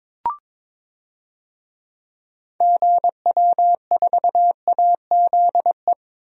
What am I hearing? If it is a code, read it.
Morse code, heard as a keyed tone, about 21 wpm, 700 Hz. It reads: GW4AZE
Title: DXCC-100 Call Signs - 6 Characters - 22wpm